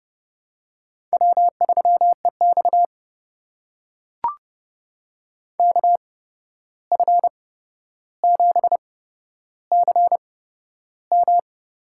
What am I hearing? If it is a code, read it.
Morse code, heard as a keyed tone, about 30 wpm, spaced as for 12 wpm, 700 Hz. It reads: W3EX KF7CM
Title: USA Call Signs - 8x Character Spacing - 30wpm